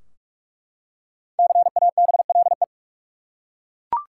Morse code, heard as a keyed tone, about 41 wpm, 700 Hz. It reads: CABLE
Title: Five Letter Words - 45wpm